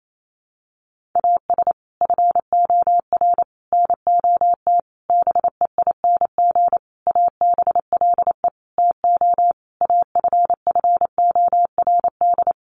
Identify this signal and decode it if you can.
Morse code, heard as a keyed tone, about 28 wpm, 700 Hz. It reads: AS FOR NOT BEING ABLE TO AFFORD
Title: Sentences from Top 1100 Words 28wpm